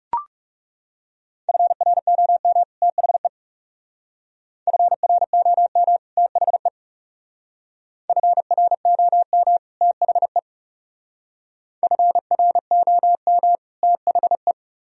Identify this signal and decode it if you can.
Morse code, heard as a keyed tone, about 35 wpm, 700 Hz. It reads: FROM THE FROM THE FROM THE FROM THE
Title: Speed-Racing - Sets of 2 Words 30wpm